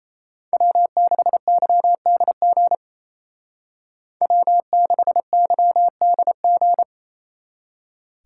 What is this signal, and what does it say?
Morse code, heard as a keyed tone, about 33 wpm, 700 Hz. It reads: W6YDG W6YDG
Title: Speed Racing - USA 1x3 Call Signs 28wpm